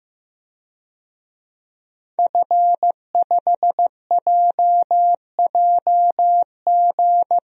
Morse code, heard as a keyed tone, about 15 wpm, 700 Hz. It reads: F5JJG
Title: DXCC-100 Call Signs - 5 Characters - Repeated 2x - 15wpm